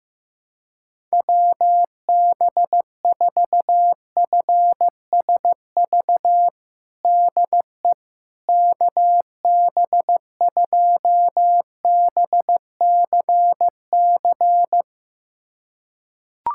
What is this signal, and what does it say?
Morse code, heard as a keyed tone, about 15 wpm, 700 Hz. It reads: WB4FSV DE KB2BCC
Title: USA Callsign DE Callsign - 6 Characters - 15wpm